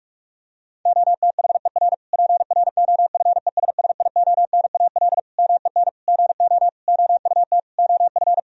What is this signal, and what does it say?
Morse code, heard as a keyed tone, about 46 wpm, 700 Hz. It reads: OTHER PROFESSIONAL MEN GO OUT OF
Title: Sentences from Top 1000 Words - Review Part 4 - 45wpm